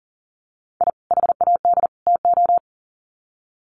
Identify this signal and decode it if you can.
Morse code, heard as a keyed tone, about 40 wpm, 700 Hz. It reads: I HAD TO
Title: Sets of 3 Words 40wpm